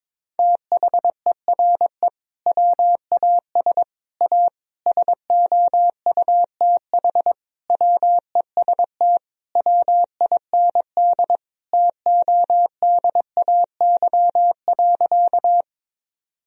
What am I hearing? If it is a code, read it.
Morse code, heard as a keyed tone, about 22 wpm, 700 Hz. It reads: THERE WAS A SOUTH WEST WIND TODAY.
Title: Sentences from Top 1000 Words - Review Part 2 - 22wpm